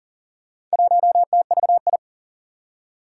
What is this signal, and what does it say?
Morse code, heard as a keyed tone, about 44 wpm, 700 Hz. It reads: 1TVI
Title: Four Challenging Letter-Number Combinations 40wpm